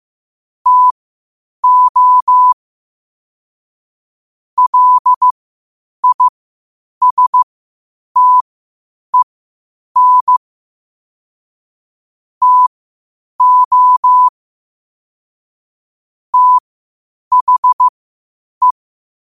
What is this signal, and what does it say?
Morse code, heard as a keyed tone, about 15 wpm, spaced as for 8 wpm, 1000 Hz. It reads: TO LISTEN TO THE
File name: qotdfiles / 8wpm